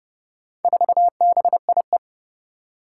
Morse code, heard as a keyed tone, about 30 wpm, 700 Hz. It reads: 4BIE
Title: Four Challenging Letter-Number Combinations 30wpm